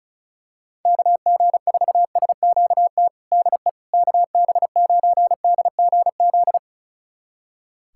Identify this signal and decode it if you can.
Morse code, heard as a keyed tone, about 36 wpm, 690 Hz. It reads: KG4SQT DE KB9DGZ
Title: USA Callsign DE Callsign - 6 Characters - 35wpm